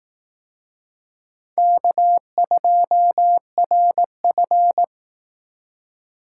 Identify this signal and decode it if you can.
Morse code, heard as a keyed tone, about 18 wpm, 700 Hz. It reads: K2RF
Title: Speed Racing - USA 1x2 Call Signs 15wpm